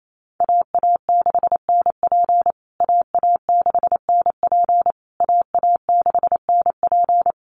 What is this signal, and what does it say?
Morse code, heard as a keyed tone, about 28 wpm, 700 Hz. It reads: AA6NP AA6NP AA6NP
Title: USA Call Signs Repeated 3x 28wpm